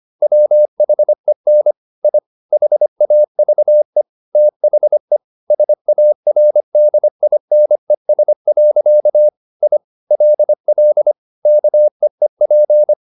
Morse code, heard as a keyed tone, about 25 wpm, 600 Hz. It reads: WHEN I HAVE THE SARDINES. I LL KEEP
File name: the old man and the sea part 1 25wpm